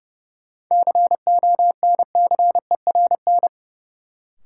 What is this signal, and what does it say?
Morse code, heard as a keyed tone, about 31 wpm, 700 Hz. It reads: CONCERN
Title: Sentences from Top 1000 Words - New Word Drill - 30wpm